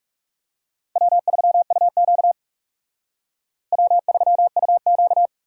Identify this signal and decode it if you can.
Morse code, heard as a keyed tone, about 37 wpm, 700 Hz. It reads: W3UQ W3UQ
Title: Speed Racing - USA 1x2 Call Signs 30wpm